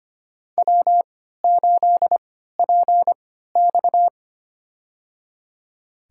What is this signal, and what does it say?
Morse code, heard as a keyed tone, about 25 wpm, spaced as for 14 wpm, 700 Hz. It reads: W8PX
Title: USA Call Signs - 4 Characters - 3x Character Spacing - 25wpm